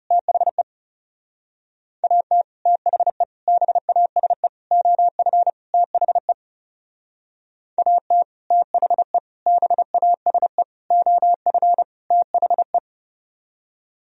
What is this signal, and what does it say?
Morse code, heard as a keyed tone, about 39 wpm, 700 Hz. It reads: THE AT THE BASE OF THE AT THE BASE OF THE